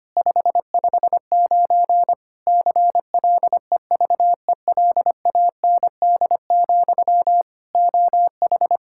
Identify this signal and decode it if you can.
Morse code, heard as a keyed tone, about 25 wpm, 700 Hz. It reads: 559 CLEVELAND, OH
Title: Practice QSO - Exchange 1 - Beginning - 25wpm